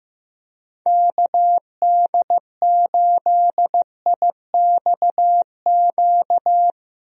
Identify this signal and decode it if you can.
Morse code, heard as a keyed tone, about 15 wpm, 700 Hz. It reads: KD8IXQ
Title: USA Call Signs - 6 Characters - Repeated 2x - 15wpm